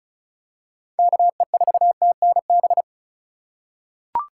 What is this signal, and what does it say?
Morse code, heard as a keyed tone, about 34 wpm, 700 Hz. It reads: KE4TNB
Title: USA Call Signs - 6 Characters - 4x Character Spacing - 35wpm